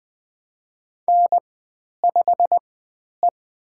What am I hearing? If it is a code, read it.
Morse code, heard as a keyed tone, about 20 wpm, spaced as for 10 wpm, 700 Hz. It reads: N5E
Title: 109 Lesson 18g - Letter m - Call Signs 20wpm